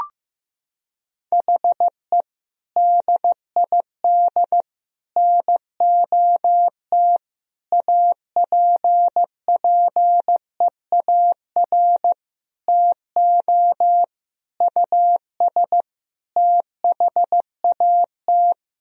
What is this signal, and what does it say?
Morse code, heard as a keyed tone, about 15 wpm, 700 Hz. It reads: HE DID NOT APPEAR TO US THAT
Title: Sentences from Top 1000 Words - Review Part 2 - 15wpm